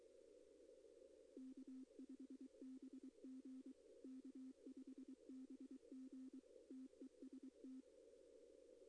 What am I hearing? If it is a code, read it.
Morse code, heard as a keyed tone, about 23 wpm, 260 Hz. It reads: K5BG K5BG TEST
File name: contests / arrl-160-k5tr-op-w5tn -2200